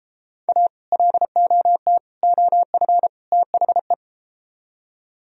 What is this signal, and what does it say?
Morse code, heard as a keyed tone, about 32 wpm, 700 Hz. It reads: A LOT OF THE